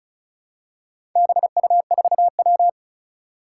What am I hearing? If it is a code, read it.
Morse code, heard as a keyed tone, about 36 wpm, 700 Hz. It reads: BU4W
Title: Four Challenging Letter-Number Combinations 35wpm